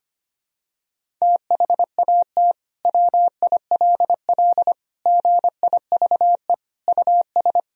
Morse code, heard as a keyed tone, about 25 wpm, 700 Hz. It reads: THAT WILL GIVE US
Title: Sentences from Top 1000 Words - Review Part 1 - 25wpm